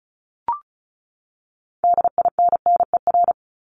Morse code, heard as a keyed tone, about 34 wpm, 700 Hz. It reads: DINNER
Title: Top 1751-2000 Words 35wpm